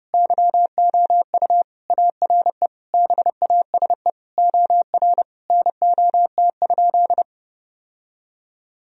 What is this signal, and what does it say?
Morse code, heard as a keyed tone, about 30 wpm, 700 Hz. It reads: YOU ARE BASE OR NOT?
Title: Sentences from Top 1000 Words - Review Part 1 - 30wpm